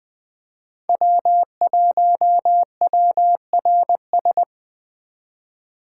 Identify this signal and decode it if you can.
Morse code, heard as a keyed tone, about 22 wpm, 700 Hz. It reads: W1WRS
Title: Speed Racing - USA 1x3 Call Signs 20wpm